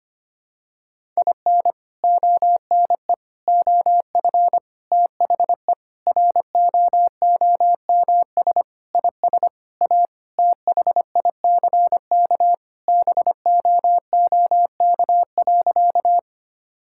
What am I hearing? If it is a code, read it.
Morse code, heard as a keyed tone, about 25 wpm, 700 Hz. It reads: IN ONE OF THE ROOMS IS A THICK BOOK.
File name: practice / 25wpm / Sentences with 4-Word Phrases - 25wpm